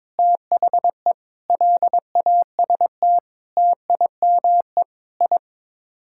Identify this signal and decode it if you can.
Morse code, heard as a keyed tone, about 21 wpm, 700 Hz. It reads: THE LAST TIME I
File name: practice / 22wpm / Sets of 7 Words - 22wpm